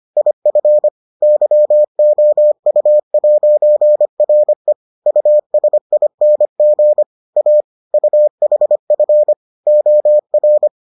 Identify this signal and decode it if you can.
Morse code, heard as a keyed tone, about 25 wpm, 600 Hz. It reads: IF YOU'RE USING A UHF OR